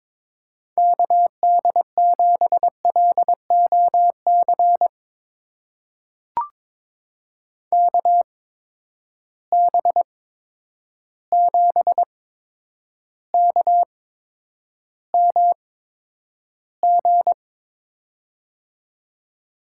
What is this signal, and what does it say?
Morse code, heard as a keyed tone, about 23 wpm, spaced as for 10 wpm, 700 Hz. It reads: KD7LOC KB7KMG
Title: USA Call Signs - 6 Characters - 8x Character Spacing - 22wpm